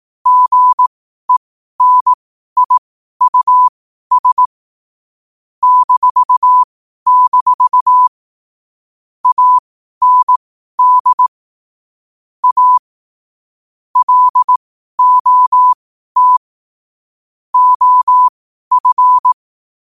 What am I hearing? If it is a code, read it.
Morse code, heard as a keyed tone, about 18 wpm, spaced as for 12 wpm, 1000 Hz. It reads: GENIUS -- AND A LOT OF
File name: qotdfiles / 12wpm